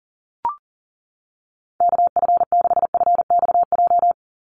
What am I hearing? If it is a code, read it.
Morse code, heard as a keyed tone, about 40 wpm, 700 Hz. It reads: KF6FXJ
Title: USA Call Signs 40wpm